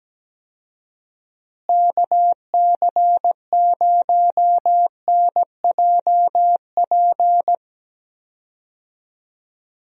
Morse code, heard as a keyed tone, about 17 wpm, 700 Hz. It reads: KC0NJP
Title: USA Call Signs - 6 Characters - Repeated 2x - 17wpm